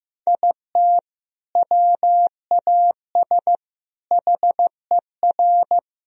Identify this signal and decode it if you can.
Morse code, heard as a keyed tone, about 15 wpm, 700 Hz. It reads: IT WAS HER
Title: Sentences from Top 1000 Words - Review Part 2 - 15wpm